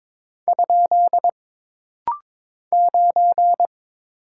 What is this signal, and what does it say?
Morse code, heard as a keyed tone, about 22 wpm, 700 Hz. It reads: ? 9
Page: This rapid-fire timing is geared to push your ICR (Instant Character Recognition) skills.